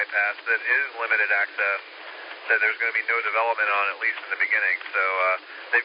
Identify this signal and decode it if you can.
Emergency Alert System (SAME) tones, heard as harsh buzzing bursts, with speech. It is heard from 1.0 to 4.8 s.